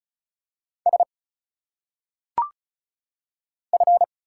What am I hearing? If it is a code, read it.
Morse code, heard as a keyed tone, about 36 wpm, 700 Hz. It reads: S F
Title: Single Letters - Phonetic Pronunciation 35wpm